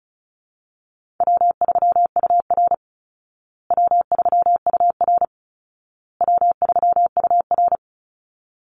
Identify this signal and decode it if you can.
Morse code, heard as a keyed tone, about 34 wpm, 700 Hz. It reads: W3UR W3UR W3UR
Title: Easy Call Signs Repeated 3x - 35wpm